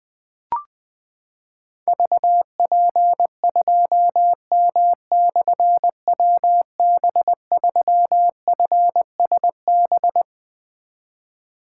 Morse code, heard as a keyed tone, about 20 wpm, 700 Hz. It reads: VP2M/WB3FSB
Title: Difficult Call Signs 20wpm